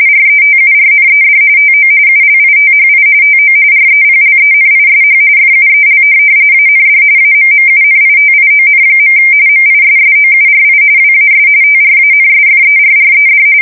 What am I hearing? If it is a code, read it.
RTTY, about 45.45 baud. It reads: RETURNED TO ULITHI TO REPLENISH FUEL AND AMMUNITION. MISSOURI WAS TRANSFERRED
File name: fsk170 / USS-Missouri